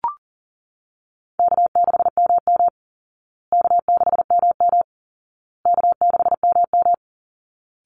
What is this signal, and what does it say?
Morse code, heard as a keyed tone, about 40 wpm, 700 Hz. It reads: K6MM K6MM K6MM
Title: Call Signs Repeated 2x - 40wpm